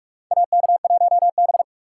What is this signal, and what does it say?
Morse code, heard as a keyed tone, about 46 wpm, 700 Hz. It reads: AK1B
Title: USA Call Signs - 4 Characters - 6x Character Spacing - 45wpm